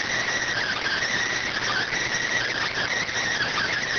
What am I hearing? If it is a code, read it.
Emergency Alert System (SAME) tones, heard as harsh buzzing bursts. It is heard from 0.5 to 3.1 s.